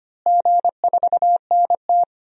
Morse code, heard as a keyed tone, about 25 wpm, 700 Hz. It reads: G4NT
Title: DXCC-100 Call Signs - 4 Characters - Repeated 2x - 25wpm